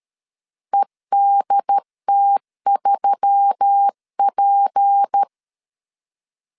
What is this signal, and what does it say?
Morse code, heard as a keyed tone, about 13 wpm, 800 Hz. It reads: EDT3P
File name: LEZIONI / 6 PKX0 / 6 P 12wpm 2033